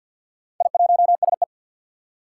Morse code, heard as a keyed tone, about 46 wpm, 700 Hz. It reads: I1SE
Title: Four Challenging Letter-Number Combinations 50wpm